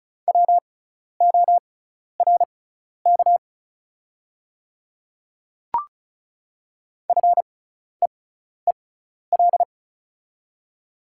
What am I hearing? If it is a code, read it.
Morse code, heard as a keyed tone, about 36 wpm, spaced as for 12 wpm, 700 Hz. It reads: WORK FEEL
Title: 6x Character Spacing - Top 500 Words - 35wpm